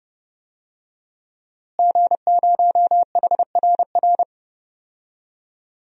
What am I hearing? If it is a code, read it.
Morse code, heard as a keyed tone, about 32 wpm, 700 Hz. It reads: G0HRR
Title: DXCC-100 Call Signs - 5 Characters - Repeated 3x - 30wpm